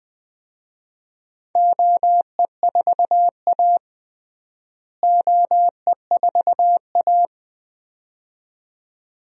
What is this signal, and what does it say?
Morse code, heard as a keyed tone, about 20 wpm, 700 Hz. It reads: OE4A OE4A
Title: DXCC-100 Call Signs - 4 Characters - Repeated 2x - 20wpm